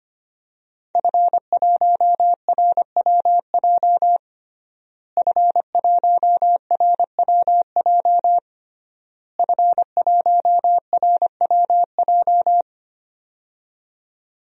Morse code, heard as a keyed tone, about 25 wpm, 700 Hz. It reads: F1RWJ F1RWJ F1RWJ
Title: DXCC-100 Call Signs - 5 Characters - Repeated 3x - 25wpm